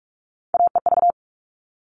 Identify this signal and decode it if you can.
Morse code, heard as a keyed tone, about 44 wpm, 700 Hz. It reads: AEV
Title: All Cut Numbers - 3-Digit 45wpm